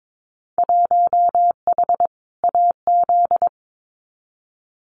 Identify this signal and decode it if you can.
Morse code, heard as a keyed tone, about 22 wpm, 700 Hz. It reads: 1H AZ
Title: Winter Field Day Exchanges 22wpm